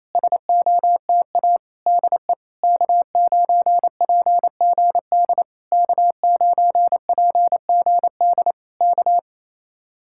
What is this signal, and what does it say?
Morse code, heard as a keyed tone, about 27 wpm, 700 Hz. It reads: SOTA DE K9PGD K9PGD K
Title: Speed-Racing Common QSO Phrases Encore - 28wpm